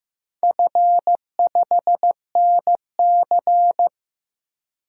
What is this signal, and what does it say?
Morse code, heard as a keyed tone, about 15 wpm, 700 Hz. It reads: F5NC
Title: DXCC-100 Call Signs - 4 Characters - 15wpm